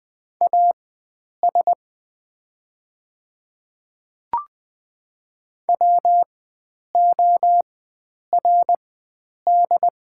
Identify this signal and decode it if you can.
Morse code, heard as a keyed tone, about 20 wpm, spaced as for 9 wpm, 700 Hz. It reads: AS WORD
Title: 4x Character Spacing - Top 100 Words - 20wpm